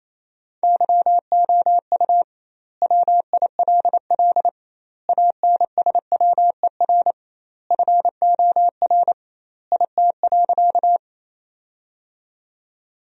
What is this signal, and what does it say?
Morse code, heard as a keyed tone, about 28 wpm, 700 Hz. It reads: YOU WILL ANSWER FOR IT.
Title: 2x Word-Spacing - Sentences from Top 300 Words - 28wpm